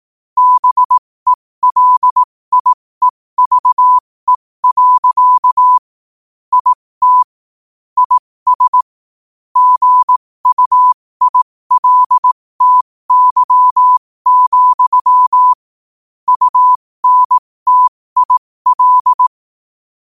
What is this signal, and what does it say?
Morse code, heard as a keyed tone, about 18 wpm, spaced as for 15 wpm, 1000 Hz. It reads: BELIEVE. IT IS GUILTY, UNTIL
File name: 15wpm